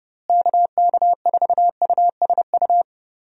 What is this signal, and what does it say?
Morse code, heard as a keyed tone, about 30 wpm, 700 Hz. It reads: KK4USU
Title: USA Call Signs - 6 Characters - Repeated 3x - 30wpm